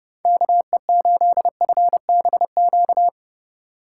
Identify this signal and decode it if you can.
Morse code, heard as a keyed tone, about 30 wpm, 700 Hz. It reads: KE8FBQ
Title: USA Call Signs - 6 Characters - 2x Character Spacing - 30wpm